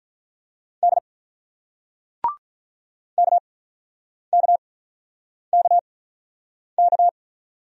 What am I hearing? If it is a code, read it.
Morse code, heard as a keyed tone, about 44 wpm, spaced as for 9 wpm, 690 Hz. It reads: D KKKK
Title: Speed-Racing Single Letters 35wpm